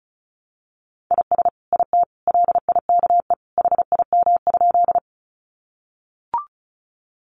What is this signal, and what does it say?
Morse code, heard as a keyed tone, about 34 wpm, 700 Hz. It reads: IS IT LIKE HIM?
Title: Sentences from Top 100 Words 35wpm